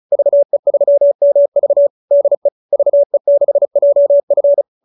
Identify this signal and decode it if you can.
Morse code, heard as a keyed tone, about 34 wpm, 570 Hz. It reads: VE3MV DE VE6JF